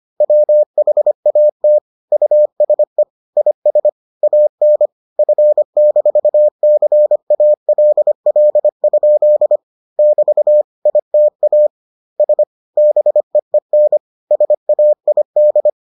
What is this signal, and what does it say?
Morse code, heard as a keyed tone, about 25 wpm, 600 Hz. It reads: WHAT USE IS AN F-CALL? <BT> ITA S BEEN SAID